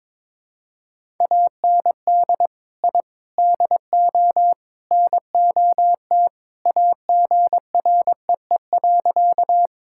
Morse code, heard as a keyed tone, about 22 wpm, 700 Hz. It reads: AND I DO NOT AGREE.
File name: practice / 22wpm / Sentences with 4-Word Phrases - 22wpm